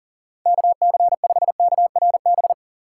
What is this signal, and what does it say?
Morse code, heard as a keyed tone, about 40 wpm, 700 Hz. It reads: KC5KRB
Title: USA Call Signs - 6 Characters - 3x Character Spacing - 40wpm